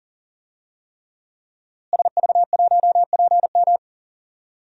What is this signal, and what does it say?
Morse code, heard as a keyed tone, about 40 wpm, 700 Hz. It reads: SV1PM
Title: DXCC-100 Call Signs - 5 Characters - Repeated 3x - 40wpm